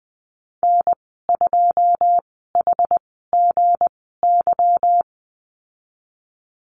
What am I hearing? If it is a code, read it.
Morse code, heard as a keyed tone, about 19 wpm, spaced as for 14 wpm, 700 Hz. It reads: N2HGY
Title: 167 Lesson 27g - Number 7 - Call Signs 20wpm